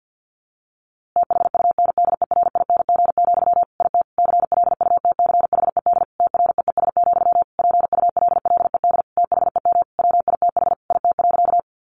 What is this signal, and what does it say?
Morse code, heard as a keyed tone, about 50 wpm, 700 Hz. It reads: THUNDERING, IT CLUTCHED TREES, PULLED THEM WITH IT.
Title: A Sound Of Thunder 50wpm